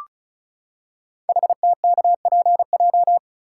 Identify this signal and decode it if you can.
Morse code, heard as a keyed tone, about 36 wpm, 700 Hz. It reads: HTKPJ
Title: Code Groups - 5 Letter:Numbers - 35wpm